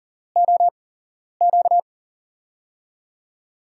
Morse code, heard as a keyed tone, about 40 wpm, spaced as for 11 wpm, 700 Hz. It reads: OQ
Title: USA Call Signs - 8x Character Spacing - 40wpm